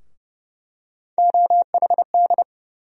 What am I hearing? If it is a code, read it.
Morse code, heard as a keyed tone, about 30 wpm, 700 Hz. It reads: OHD
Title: USA Call Sign Postfixes - 3 Letters 30wpm